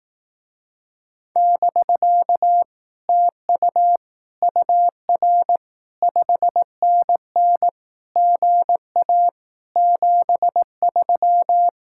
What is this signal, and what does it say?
Morse code, heard as a keyed tone, about 18 wpm, 700 Hz. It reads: <BK> TU UR 5NN GA 73
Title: Speed-Racing Common QSO Phrases Encore - 15wpm